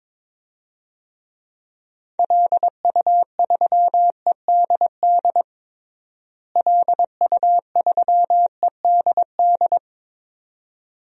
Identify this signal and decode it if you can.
Morse code, heard as a keyed tone, about 22 wpm, 700 Hz. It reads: LU3EDD LU3EDD
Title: DXCC-100 Call Signs - 6 Characters - Repeated 2x - 22wpm